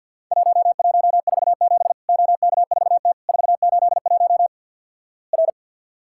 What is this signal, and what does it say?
Morse code, heard as a keyed tone, about 50 wpm, 700 Hz. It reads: 1147 OK4T 481 R
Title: Worked All Europe DX Contest - QTC Exchange - 50wpm